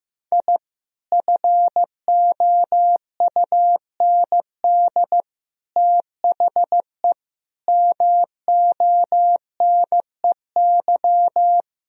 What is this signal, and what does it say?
Morse code, heard as a keyed tone, about 15 wpm, 700 Hz. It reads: I FOUND THE MONEY
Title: Sentences from Top 1000 Words - Review Part 3 - 15wpm